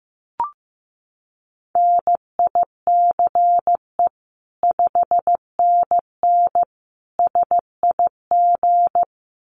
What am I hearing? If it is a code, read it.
Morse code, heard as a keyed tone, about 15 wpm, 700 Hz. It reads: NICE 5NN SIG
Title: Common QSO Phrases 15wpm